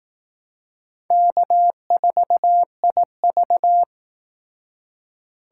Morse code, heard as a keyed tone, about 18 wpm, 700 Hz. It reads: K4IV
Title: Speed Racing - USA 1x2 Call Signs 15wpm